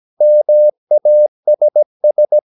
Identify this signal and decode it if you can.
Morse code, heard as a keyed tone, about 17 wpm, 600 Hz. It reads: MASS
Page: Romanen Never af Ken Follet på engelsk.